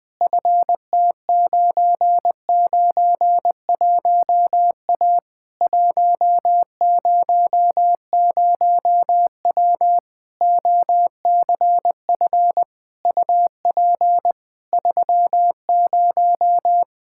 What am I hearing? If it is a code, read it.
Morse code, heard as a keyed tone, about 21 wpm, 700 Hz. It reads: FT991A 100W OCF UP 30
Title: Practice QSO - Exchange 2 - Info - 20wpm